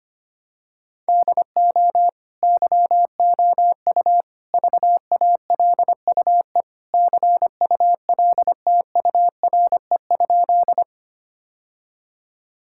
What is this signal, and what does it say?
Morse code, heard as a keyed tone, about 25 wpm, 700 Hz. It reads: DO YOU VALUE CULTURE?
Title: Sentences from Top 1000 Words - Review Part 6 - 25wpm